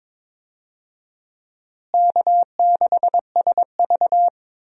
Morse code, heard as a keyed tone, about 21 wpm, 700 Hz. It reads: K6SV
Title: USA Call Signs - 4 Characters - Repeated 2x - 22wpm